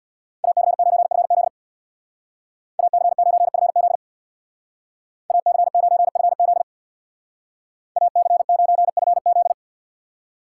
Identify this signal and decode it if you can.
Morse code, heard as a keyed tone, about 59 wpm, 700 Hz. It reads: AC9FZ AC9FZ AC9FZ AC9FZ
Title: Speed Racing - USA Call Signs 50wpm